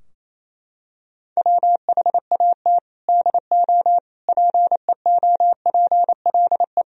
Morse code, heard as a keyed tone, about 28 wpm, 700 Hz. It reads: WHAT DO PEOPLE
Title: Sentences from Top 1000 Words - Review Part 4 - 28wpm